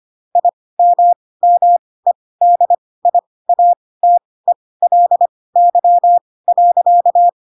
Morse code, heard as a keyed tone, about 25 wpm, spaced as for 18 wpm, 700 Hz. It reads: IMMEDIATELY.
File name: Pinocchio / Pinocchio ch1 12 25 ik1hge